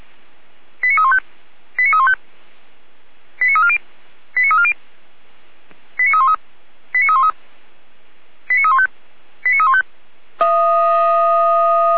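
Five-tone selective call (ZVEI), heard as quick runs of five tones, tones of about 70 ms. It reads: 89315 89315 89340 89340 89313 89313 89315 89315
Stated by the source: ZVEI five-tone selective call received by SDR.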